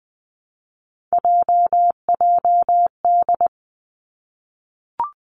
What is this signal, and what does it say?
Morse code, heard as a keyed tone, about 20 wpm, 700 Hz. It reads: JJD
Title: Three Challenging Letter-Number Combinations 20wpm